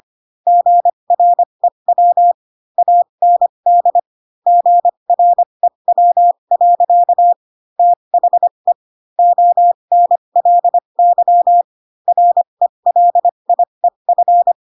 Morse code, heard as a keyed tone, about 25 wpm, spaced as for 22 wpm, 700 Hz. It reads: GREW AND GREW. THE ONLY RELIEF